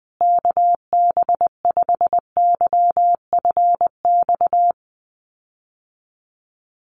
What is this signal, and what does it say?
Morse code, heard as a keyed tone, about 20 wpm, 700 Hz. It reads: KB5YFX
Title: USA Call Signs Repeated 3x 20wpm